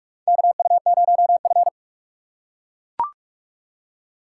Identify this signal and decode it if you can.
Morse code, heard as a keyed tone, about 43 wpm, 700 Hz. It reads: KU0F
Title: USA Call Signs - 4 Characters - 2x Character Spacing - 45wpm